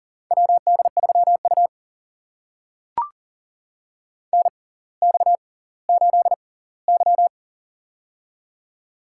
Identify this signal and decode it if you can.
Morse code, heard as a keyed tone, about 40 wpm, spaced as for 21 wpm, 700 Hz. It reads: WD3U NX8Y
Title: USA Call Signs - 4 Characters - 6x Character Spacing - 40wpm